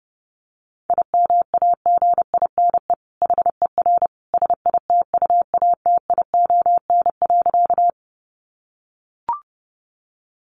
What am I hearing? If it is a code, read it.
Morse code, heard as a keyed tone, about 30 wpm, 700 Hz. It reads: IMAGINE HER SITUATION.
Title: Sentences from Top 1400 Words 30wpm